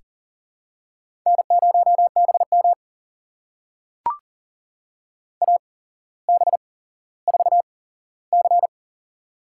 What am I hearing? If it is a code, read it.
Morse code, heard as a keyed tone, about 40 wpm, spaced as for 17 wpm, 700 Hz. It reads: N0BM AB4C
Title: USA Call Signs - 4 Characters - 8x Character Spacing - 40wpm